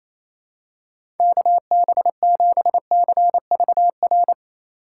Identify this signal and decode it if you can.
Morse code, heard as a keyed tone, about 28 wpm, 700 Hz. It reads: KB7CVR
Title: USA Call Signs - 3x Character Spacing - 28wpm